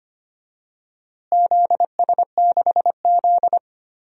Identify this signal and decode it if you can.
Morse code, heard as a keyed tone, about 25 wpm, 700 Hz. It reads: ZS6Z
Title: DXCC-100 Call Signs - 4 Characters - Repeated 3x - 25wpm